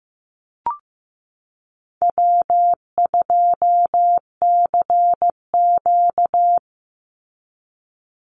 Morse code, heard as a keyed tone, about 15 wpm, 700 Hz. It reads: W2CQ
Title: Easy Call Signs Repeated 2x - 15wpm